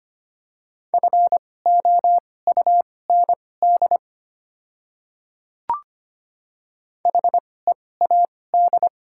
Morse code, heard as a keyed tone, about 25 wpm, spaced as for 18 wpm, 700 Hz. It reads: FOUND HEAD
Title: 2x Character Spacing - Top 300 Words - 25wpm